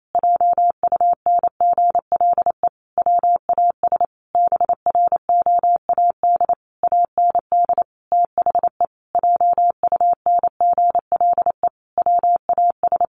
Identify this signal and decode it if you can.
Morse code, heard as a keyed tone, about 28 wpm, 700 Hz. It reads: JUNGLE WAS BROAD AND THE JUNGLE WAS
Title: A Sound Of Thunder 28wpm